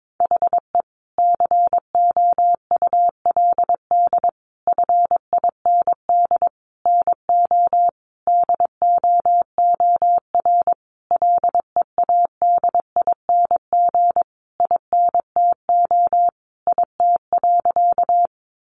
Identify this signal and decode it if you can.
Morse code, heard as a keyed tone, about 22 wpm, 700 Hz. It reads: HE COULD FIND NO DOOR LEADING INTO IT.
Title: Sentences from Top 1100 Words 22wpm